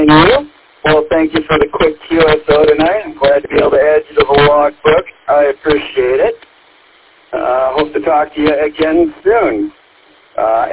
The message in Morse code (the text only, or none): none